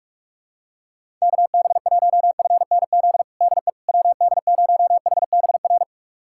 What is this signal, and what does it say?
Morse code, heard as a keyed tone, about 46 wpm, 700 Hz. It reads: KB1FNZ DE WD0HBR
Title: USA Callsign DE Callsign - 6 Characters - 45wpm